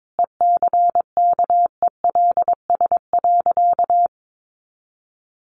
Morse code, heard as a keyed tone, about 22 wpm, 700 Hz. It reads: ECKELS.
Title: A Sound Of Thunder 22wpm